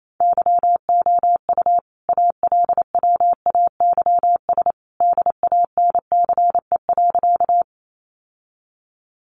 Morse code, heard as a keyed tone, about 28 wpm, 700 Hz. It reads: YOU ALWAYS DANCE.
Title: Sentences from Top 700 Words 28wpm